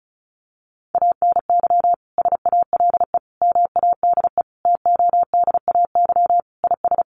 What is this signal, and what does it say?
Morse code, heard as a keyed tone, about 34 wpm, 700 Hz. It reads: ANY SALE MADE TODAY IS
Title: Sentences from Top 1400 Words 35wpm